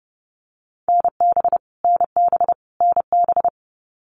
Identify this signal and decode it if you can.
Morse code, heard as a keyed tone, about 30 wpm, 700 Hz. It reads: NB NB NB
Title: Canadian Provinces and Territories - Repeated 3x - 30wpm